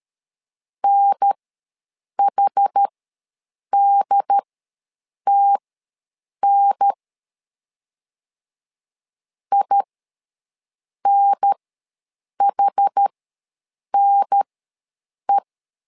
Morse code, heard as a keyed tone, about 13 wpm, spaced as for 7 wpm, 800 Hz. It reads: NHDTN INHNE
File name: LEZIONI / 2 UHDN / 2 UHDN ws 12wpm 1708